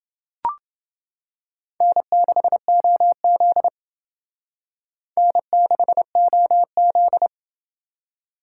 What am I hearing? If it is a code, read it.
Morse code, heard as a keyed tone, about 28 wpm, 700 Hz. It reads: N6OZ N6OZ